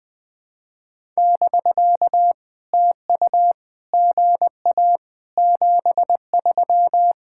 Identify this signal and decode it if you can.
Morse code, heard as a keyed tone, about 21 wpm, 700 Hz. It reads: <BK> TU GA 73
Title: POTA QSO Practice - 20wpm